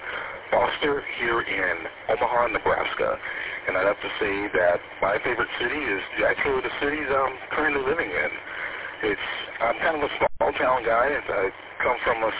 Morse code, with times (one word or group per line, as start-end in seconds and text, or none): none